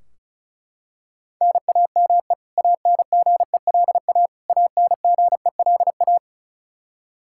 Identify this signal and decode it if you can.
Morse code, heard as a keyed tone, about 36 wpm, 700 Hz. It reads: NAME ANGELA ANGELA
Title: Practice QSO - Exchange 1 - Beginning - 35wpm